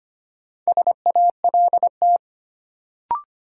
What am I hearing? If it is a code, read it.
Morse code, heard as a keyed tone, about 25 wpm, 700 Hz. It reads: SALT
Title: Sentences from Top 800 Words - New Word Drill - 25wpm